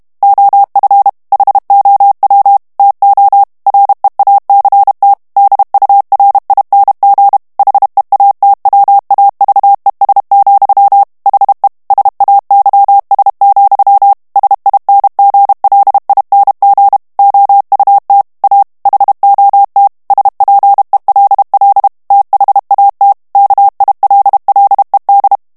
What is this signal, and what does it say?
Morse code, heard as a keyed tone, about 32 wpm, 800 Hz. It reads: OF HOW TO REACT DURING HEATWAVES, HE SAYS, SINGLING OUT A HOT SPELL THAT KILLED
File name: ANGLAIS texte clair 2 vitesse 30 mots minute